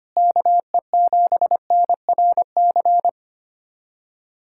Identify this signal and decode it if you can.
Morse code, heard as a keyed tone, about 25 wpm, 700 Hz. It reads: KE7NRC
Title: USA Callsign DE Callsign - 6 Characters - 25wpm